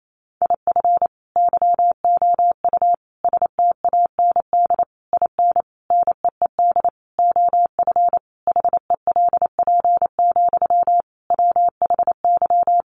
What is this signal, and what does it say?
Morse code, heard as a keyed tone, about 28 wpm, 700 Hz. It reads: IF YOU STAND IN NEED OF HELP, WHY
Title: Sentences from Top 1200 Words 28wpm